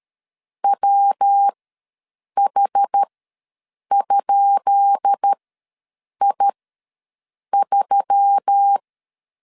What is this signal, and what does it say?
Morse code, heard as a keyed tone, about 13 wpm, spaced as for 7 wpm, 800 Hz. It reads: WH?I3
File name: LEZIONI / 10 SIGNS / 10D COMA ws 12wpm 3618